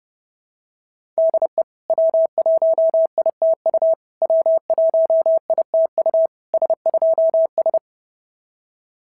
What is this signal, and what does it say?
Morse code, heard as a keyed tone, about 28 wpm, 650 Hz. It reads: DE W1ITU W1ITU S2S
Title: SOTA QSO Practice 30wpm